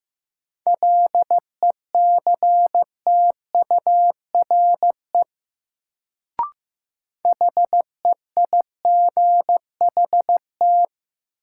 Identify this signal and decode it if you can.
Morse code, heard as a keyed tone, about 15 wpm, 700 Hz. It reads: LECTURE HEIGHT
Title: Sentences from Top 1600 Words - New Word Drill - 15wpm